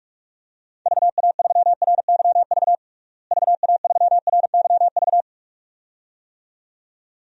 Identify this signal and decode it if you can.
Morse code, heard as a keyed tone, about 46 wpm, 700 Hz. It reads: VA3RYV VA3RYV
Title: DXCC-100 Call Signs - 6 Characters - Repeated 2x - 45wpm